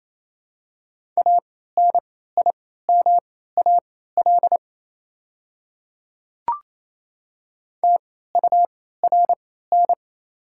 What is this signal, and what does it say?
Morse code, heard as a keyed tone, about 28 wpm, spaced as for 16 wpm, 690 Hz. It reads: ANIMAL TURN
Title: 3x Character Spacing - Top 500 Words - 28wpm